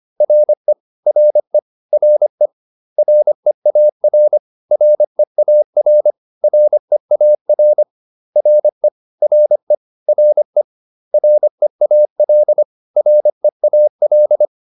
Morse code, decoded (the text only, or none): RE RE RE REAR REAR REAR RE RE RE REAL REAL